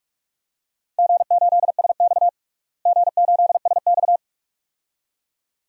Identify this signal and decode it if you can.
Morse code, heard as a keyed tone, about 41 wpm, 700 Hz. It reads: G8SX G8SX
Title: DXCC-100 Call Signs - 4 Characters - Repeated 2x - 45wpm